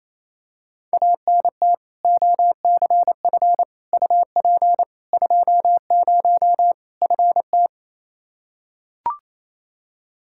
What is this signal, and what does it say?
Morse code, heard as a keyed tone, about 28 wpm, 700 Hz. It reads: ANT OCF UP 20 FT
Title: Practice QSO - Exchange 2 - Info - 28wpm